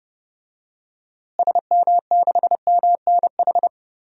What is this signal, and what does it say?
Morse code, heard as a keyed tone, about 30 wpm, 700 Hz. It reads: SM6MNH
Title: DXCC-100 Call Signs - 6 Characters - Repeated 2x - 30wpm